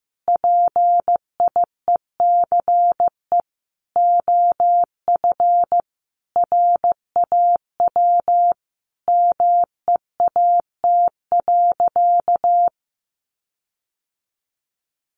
Morse code, heard as a keyed tone, about 15 wpm, 700 Hz. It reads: PIECE OF RAW MEAT.
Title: Sentences from Top 1500 Words 15wpm